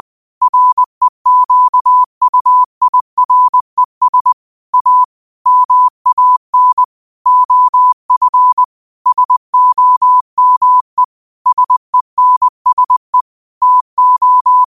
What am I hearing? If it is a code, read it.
Morse code, heard as a keyed tone, about 20 wpm, 1000 Hz. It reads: REQUIRES A MAN OF SOME SENSE TO
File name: qotdfiles / 20wpm